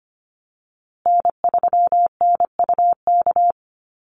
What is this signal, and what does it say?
Morse code, heard as a keyed tone, about 25 wpm, 700 Hz. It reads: N3NUK
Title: USA Call Signs Repeated 2x 25wpm